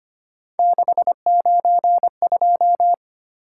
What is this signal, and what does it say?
Morse code, heard as a keyed tone, about 25 wpm, 700 Hz. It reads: 692
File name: Three Letter-Number Combinations - 25wpm